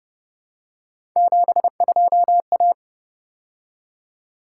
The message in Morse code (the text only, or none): 72A